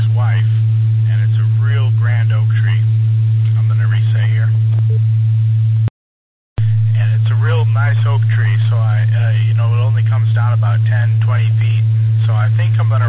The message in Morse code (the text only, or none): none